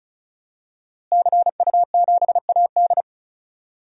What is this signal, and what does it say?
Morse code, heard as a keyed tone, about 36 wpm, 700 Hz. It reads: CU7AD
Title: DXCC-100 Call Signs - 5 Characters - Repeated 3x - 35wpm